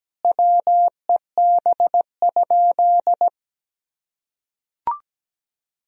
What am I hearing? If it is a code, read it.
Morse code, heard as a keyed tone, about 17 wpm, 700 Hz. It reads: WEB?
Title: Sentences from Top 1000 Words - Review Part 4 - 17wpm